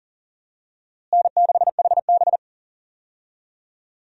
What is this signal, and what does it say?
Morse code, heard as a keyed tone, about 40 wpm, 700 Hz. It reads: N6HB
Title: USA Call Signs - 4 Characters - Repeated 2x - 40wpm